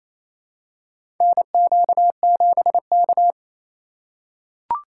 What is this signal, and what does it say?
Morse code, heard as a keyed tone, about 27 wpm, 700 Hz. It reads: NQ7K
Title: USA Call Signs - 4 Characters - 4x Character Spacing - 28wpm